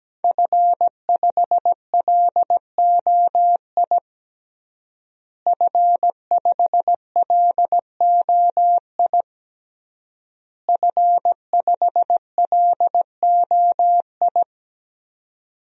Morse code, heard as a keyed tone, about 17 wpm, 700 Hz. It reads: F5LOI F5LOI F5LOI
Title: DXCC-100 Call Signs - 5 Characters - Repeated 3x - 17wpm